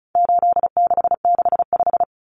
Morse code, heard as a keyed tone, about 34 wpm, 700 Hz. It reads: 8665
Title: Common Cut Numbers - 4-Digit 35wpm